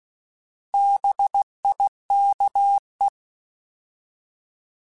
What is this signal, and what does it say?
Morse code, heard as a keyed tone, about 16 wpm, 800 Hz. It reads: BIKE
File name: Alphabetical / Alphabetical mp3s / Bs @15